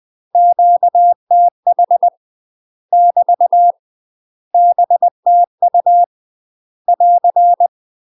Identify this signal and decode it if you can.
Morse code, heard as a keyed tone, about 20 wpm, 700 Hz. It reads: QTH <BT> BTU <AR>
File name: 20WPM